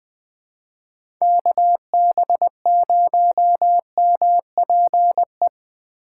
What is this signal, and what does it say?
Morse code, heard as a keyed tone, about 20 wpm, 700 Hz. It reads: KB0MPE
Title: USA Call Signs - 6 Characters - 6x Character Spacing - 20wpm